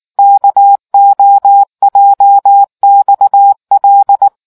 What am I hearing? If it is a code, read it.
Morse code, heard as a keyed tone, about 19 wpm, 800 Hz. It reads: KOJXL